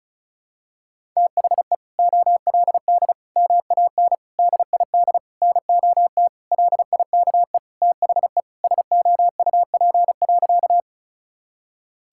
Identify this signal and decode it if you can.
Morse code, heard as a keyed tone, about 34 wpm, 700 Hz. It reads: THE OLD MAN DID NOT LIKE THE SOUP.
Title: Sentences from Top 1000 Words - Review Part 6 - 35wpm